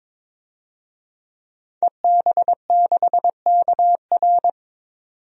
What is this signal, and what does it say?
Morse code, heard as a keyed tone, about 22 wpm, 700 Hz. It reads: EB6KR
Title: DXCC-100 Call Signs - 5 Characters - 22wpm